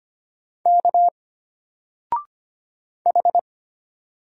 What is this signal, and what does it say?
Morse code, heard as a keyed tone, about 25 wpm, 700 Hz. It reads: K H
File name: Single-Letter-Number-25wpm